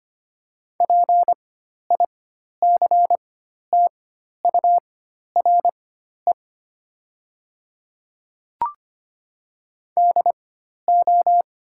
Morse code, heard as a keyed tone, about 25 wpm, spaced as for 12 wpm, 700 Hz. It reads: PICTURE DO
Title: 4x Character Spacing - Top 200 Words - 25wpm